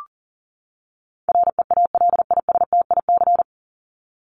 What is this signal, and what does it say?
Morse code, heard as a keyed tone, about 40 wpm, 700 Hz. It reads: REALISTIC
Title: Top 1751-2000 Words 40wpm